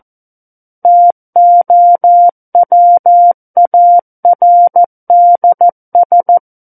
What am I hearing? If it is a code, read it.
Morse code, heard as a keyed tone, about 14 wpm, 700 Hz. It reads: TOWARDS
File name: Book One / 05-Chapter 5 - 14 WPM